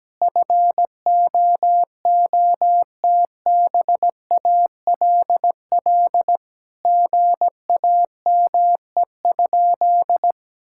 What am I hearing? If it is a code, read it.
Morse code, heard as a keyed tone, about 17 wpm, 700 Hz. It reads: FOOTBALL GAME?
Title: Sentences from Top 1600 Words 17wpm